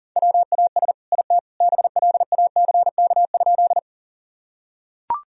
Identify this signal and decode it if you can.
Morse code, heard as a keyed tone, about 40 wpm, 700 Hz. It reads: WAS IT BLACK?
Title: Sentences from Top 1000 Words - Review Part 2 - 40wpm